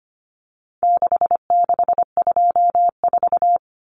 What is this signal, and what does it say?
Morse code, heard as a keyed tone, about 25 wpm, 700 Hz. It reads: 6624
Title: Numbers - 4-Digit 25wpm